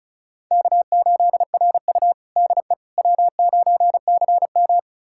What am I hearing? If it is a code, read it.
Morse code, heard as a keyed tone, about 36 wpm, 700 Hz. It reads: K8RU DE W9CM
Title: USA Callsign DE Callsign - 4 Characters - 35wpm